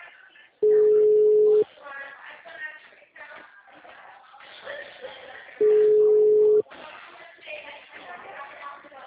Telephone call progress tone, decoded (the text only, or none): ringback tone